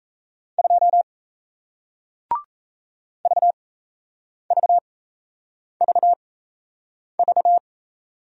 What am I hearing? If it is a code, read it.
Morse code, heard as a keyed tone, about 38 wpm, spaced as for 8 wpm, 700 Hz. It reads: 2 VVVV